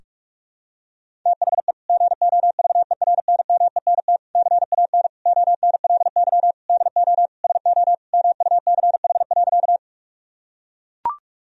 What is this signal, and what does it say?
Morse code, heard as a keyed tone, about 41 wpm, 700 Hz. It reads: THE GOVERNMENT CAN ONLY DO SO MUCH.
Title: Sentences from Top 1000 Words - Review Part 3 - 45wpm